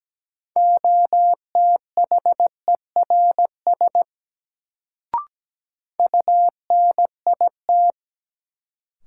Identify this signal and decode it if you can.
Morse code, heard as a keyed tone, about 17 wpm, 700 Hz. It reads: OTHERS UNIT
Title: Sentences from Top 1000 Words - Review Part 2 - New Word Drill - 17wpm